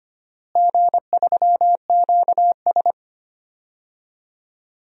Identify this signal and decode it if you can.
Morse code, heard as a keyed tone, about 25 wpm, 700 Hz. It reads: G3QS
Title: DXCC-100 Call Signs - 4 Characters - Repeated 3x - 25wpm